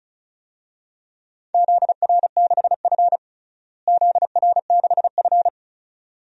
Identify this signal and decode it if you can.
Morse code, heard as a keyed tone, about 36 wpm, 700 Hz. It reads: ZR6F ZR6F
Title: DXCC-100 Call Signs - 4 Characters - Repeated 2x - 35wpm